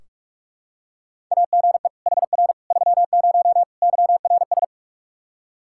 Morse code, heard as a keyed tone, about 46 wpm, 700 Hz. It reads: AGE HR 30 YRS
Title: Practice QSO - Exchange 3 - More Info - 45wpm